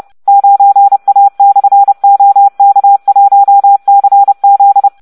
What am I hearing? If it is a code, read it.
Morse code, heard as a keyed tone, about 30 wpm, 800 Hz. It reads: 9A/OK1CZ